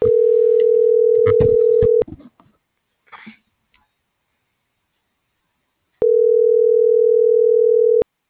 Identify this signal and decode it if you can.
Ringback tone (North America), 440 + 480 Hz.